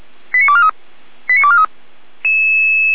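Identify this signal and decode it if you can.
Five-tone selective call (ZVEI), heard as quick runs of five tones, tones of about 80 ms. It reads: 89243 89243
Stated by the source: Five-tone ZVEI selective call received by SDR, archived by the fire brigade Feuerwehr Orlamünde.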